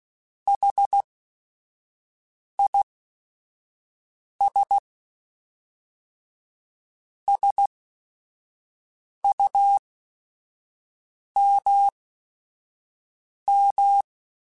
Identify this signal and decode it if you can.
Morse code, heard as a keyed tone, about 16 wpm, spaced as for 5 wpm, 800 Hz. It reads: HIS SUMM